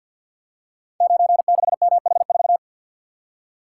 Morse code, heard as a keyed tone, about 52 wpm, 700 Hz. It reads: 96MH4
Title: Code Groups - 5 Letter:Numbers - 50wpm